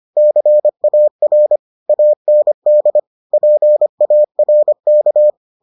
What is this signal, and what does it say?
Morse code, heard as a keyed tone, about 25 wpm, 600 Hz. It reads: CAR AND PARK